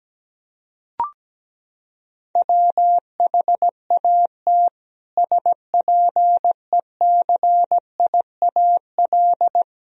Morse code, heard as a keyed tone, about 17 wpm, 700 Hz. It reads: WHAT SPECIAL
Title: Sentences from Top 1000 Words - Review Part 3 - 17wpm